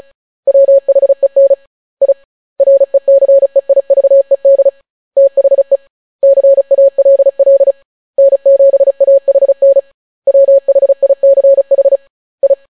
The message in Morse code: WHEN I RECEIVED THE CALL N7AHN WHICH I

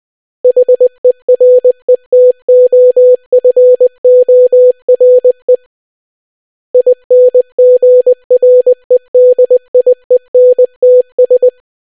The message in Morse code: HERETOFORE INGREDIENTS

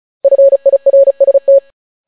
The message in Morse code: FIRST

